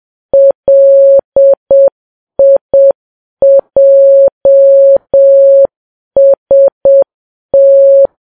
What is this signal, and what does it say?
Morse code, heard as a keyed tone, about 7 wpm, 550 Hz. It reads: LIJST